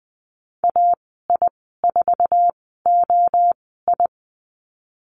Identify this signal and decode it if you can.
Morse code, heard as a keyed tone, about 20 wpm, spaced as for 15 wpm, 700 Hz. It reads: AI4OI
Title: 055 Lesson 10g - Letter r - Call Signs 20wpm